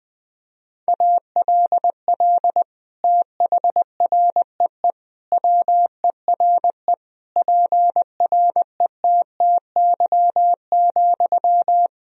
Morse code, heard as a keyed tone, about 21 wpm, 700 Hz. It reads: ALL THREE WERE PRETTY,
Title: Sentences from Top 1000 Words - Review Part 4 - 20wpm